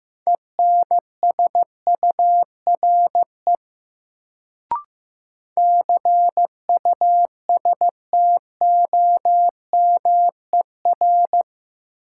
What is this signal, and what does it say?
Morse code, heard as a keyed tone, about 15 wpm, 700 Hz. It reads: ENSURE CUSTOMER
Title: Sentences from Top 1100 Words - New Word Drill Alt - 15wpm